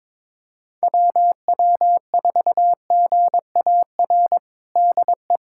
Morse code, heard as a keyed tone, about 22 wpm, 700 Hz. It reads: WW4GAR DE
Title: USA Callsign DE Callsign - 6 Characters - 22wpm